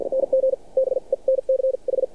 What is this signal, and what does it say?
Morse code, heard as a keyed tone, about 55 wpm, spaced as for 42 wpm, 530 Hz. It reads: NG BENCH
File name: archived / CRSnet